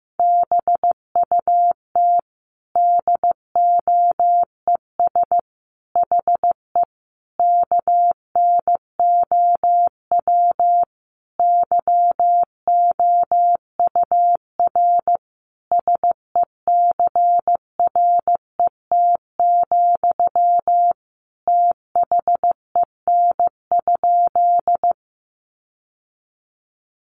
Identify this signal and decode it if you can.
Morse code, heard as a keyed tone, about 15 wpm, 700 Hz. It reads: BUT DOES HE KNOW YOUR SECRET, THEN?
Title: Sentences from Top 1200 Words 15wpm